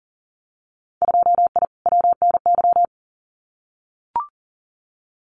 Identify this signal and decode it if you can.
Morse code, heard as a keyed tone, about 40 wpm, 700 Hz. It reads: 2I WNY
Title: Winter Field Day Exchanges 40wpm